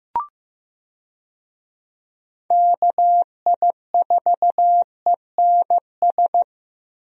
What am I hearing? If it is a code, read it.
Morse code, heard as a keyed tone, about 15 wpm, 700 Hz. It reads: KI4ENS
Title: USA Call Signs - 6 Characters - Repeated 3x - 15wpm